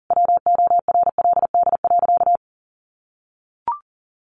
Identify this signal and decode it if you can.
Morse code, heard as a keyed tone, about 40 wpm, 700 Hz. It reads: WORLD.
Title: Sentences from Top 1100 Words 40wpm